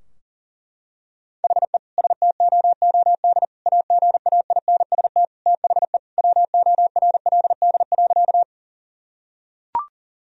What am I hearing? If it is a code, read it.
Morse code, heard as a keyed tone, about 39 wpm, 700 Hz. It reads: HE STOOD AGAINST THE WORLD.
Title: Sentences from Top 1000 Words - Review Part 2 - 40wpm